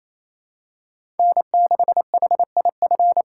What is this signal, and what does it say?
Morse code, heard as a keyed tone, about 27 wpm, 700 Hz. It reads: N6HIF
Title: USA Call Signs - 5 Characters - 2x Character Spacing - 28wpm